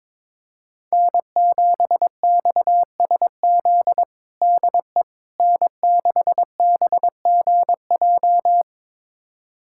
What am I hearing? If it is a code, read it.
Morse code, heard as a keyed tone, about 22 wpm, 700 Hz. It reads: N7XSZ DE N6BGJ
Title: USA Callsign DE Callsign - 5 Characters - 22wpm